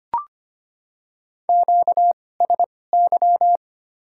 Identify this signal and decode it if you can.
Morse code, heard as a keyed tone, about 25 wpm, spaced as for 18 wpm, 700 Hz. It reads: QSY
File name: Top 90 QSO Elements - 2x Character Spacing - 25wpm